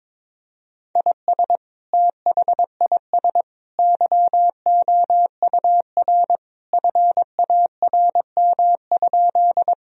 Morse code, heard as a keyed tone, about 22 wpm, 700 Hz. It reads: IS THIS YOUR FARM?
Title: Sentences from Top 1000 Words - Review Part 2 - 22wpm